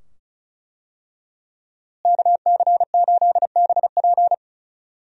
Morse code, heard as a keyed tone, about 34 wpm, 700 Hz. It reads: KC8BP
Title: USA Call Signs - 5 Characters - Repeated 2x - 35wpm